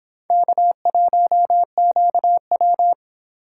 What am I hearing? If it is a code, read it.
Morse code, heard as a keyed tone, about 25 wpm, 700 Hz. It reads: K1QW